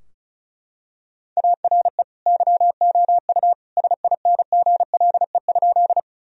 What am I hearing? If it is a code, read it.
Morse code, heard as a keyed tone, about 34 wpm, 700 Hz. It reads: ARE YOU SINGLE?
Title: Sentences from Top 1000 Words - Review Part 4 - 35wpm